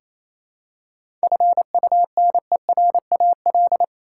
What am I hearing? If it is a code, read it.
Morse code, heard as a keyed tone, about 28 wpm, 700 Hz. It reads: FUNERAL
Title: Seven Letter Words - 28wpm